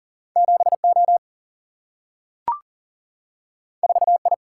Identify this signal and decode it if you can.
Morse code, heard as a keyed tone, about 40 wpm, 700 Hz. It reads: 7O 4I
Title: Single-Letter Single-Number 40wpm